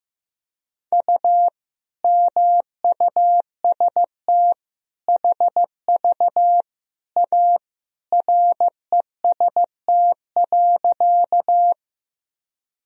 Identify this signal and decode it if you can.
Morse code, heard as a keyed tone, about 15 wpm, 700 Hz. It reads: U MUST HV A REST.
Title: Sentences from Top 500 Words and Abbreviations 15wpm